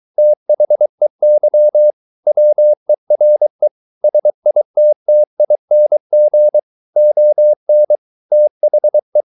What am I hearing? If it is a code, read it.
Morse code, heard as a keyed tone, about 23 wpm, 600 Hz. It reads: THEY WERE SITTING ON THE